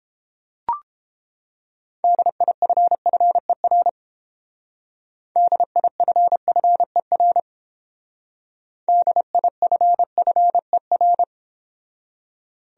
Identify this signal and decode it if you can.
Morse code, heard as a keyed tone, about 30 wpm, 700 Hz. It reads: DIFFER DIFFER DIFFER